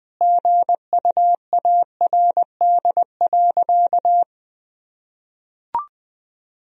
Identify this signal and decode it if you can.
Morse code, heard as a keyed tone, about 19 wpm, 700 Hz. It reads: GUARD.
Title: Sentences from Top 1000 Words - Review Part 4 - 20wpm